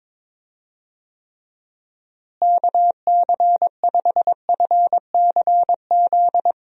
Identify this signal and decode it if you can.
Morse code, heard as a keyed tone, about 22 wpm, 700 Hz. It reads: KC5FCZ
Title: USA Call Signs - 6 Characters - Repeated 2x - 22wpm